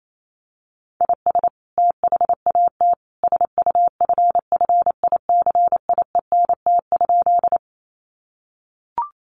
Morse code, heard as a keyed tone, about 28 wpm, 700 Hz. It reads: IS THAT SUFFICIENT?
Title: Sentences from Top 1400 Words 28wpm